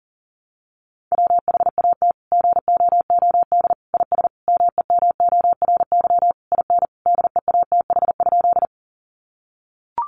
Morse code, heard as a keyed tone, about 40 wpm, 700 Hz. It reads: WHAT GOOD IS MEMORY IN DEATH?
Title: Sentences from Top 1300 Words 40wpm